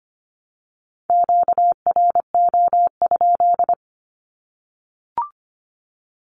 Morse code, heard as a keyed tone, about 25 wpm, 700 Hz. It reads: QRO?
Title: Q-Codes 25wpm